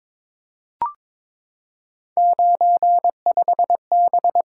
Morse code, heard as a keyed tone, about 22 wpm, 700 Hz. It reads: 95B
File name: practice / 22wpm / Three Letter-Number Combinations - 22wpm